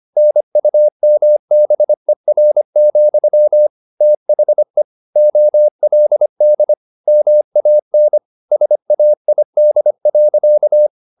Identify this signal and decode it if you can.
Morse code, heard as a keyed tone, about 25 wpm, 600 Hz. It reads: NUMBER, THE OLD MAN SAID.
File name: the old man and the sea part 1 25wpm